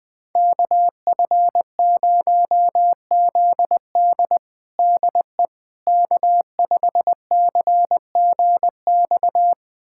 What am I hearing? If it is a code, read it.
Morse code, heard as a keyed tone, about 21 wpm, 700 Hz. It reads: KF0ZD DE K5CGX
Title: USA Callsign DE Callsign - 5 Characters - 20wpm